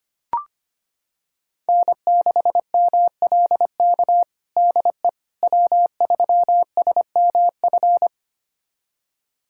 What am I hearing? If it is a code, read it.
Morse code, heard as a keyed tone, about 25 wpm, 700 Hz. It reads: N6MLK DE W3SMF
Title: USA Callsign DE Callsign - 5 Characters - 25wpm